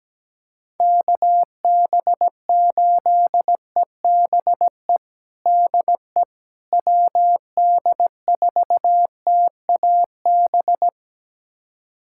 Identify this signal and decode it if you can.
Morse code, heard as a keyed tone, about 17 wpm, 700 Hz. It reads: KB8EBE DE WD4TAB
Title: USA Callsign DE Callsign - 6 Characters - 17wpm